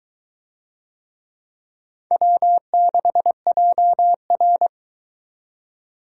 Morse code, heard as a keyed tone, about 22 wpm, 700 Hz. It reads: W6JR